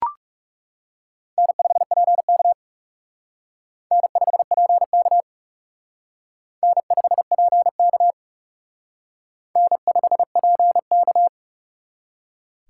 Morse code, heard as a keyed tone, about 37 wpm, 700 Hz. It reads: N5PK N5PK N5PK N5PK
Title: Speed Racing - USA 1x2 Call Signs 30wpm